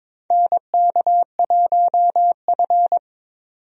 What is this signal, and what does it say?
Morse code, heard as a keyed tone, about 22 wpm, 700 Hz. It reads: NK1F
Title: USA Call Signs - 4 Characters - 22wpm